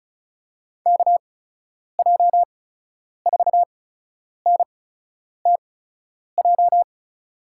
Morse code, heard as a keyed tone, about 36 wpm, spaced as for 10 wpm, 700 Hz. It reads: KJ4NTJ
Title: USA Call Signs - 6 Characters - 8x Character Spacing - 35wpm